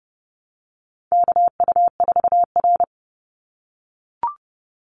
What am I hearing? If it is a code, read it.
Morse code, heard as a keyed tone, about 30 wpm, 700 Hz. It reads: KU4R
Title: NA SOTA Chaser Callsigns 30wpm